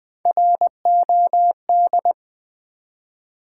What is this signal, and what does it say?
Morse code, heard as a keyed tone, about 20 wpm, 700 Hz. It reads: ROD
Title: Build A Word - 1 Letter At A Time - 3-Letter Words - 20wpm